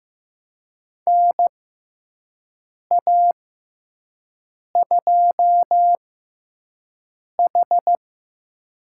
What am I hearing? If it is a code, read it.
Morse code, heard as a keyed tone, about 15 wpm, spaced as for 5 wpm, 700 Hz. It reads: NA2H
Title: USA Call Signs - 4 Characters - 6x Character Spacing - 15wpm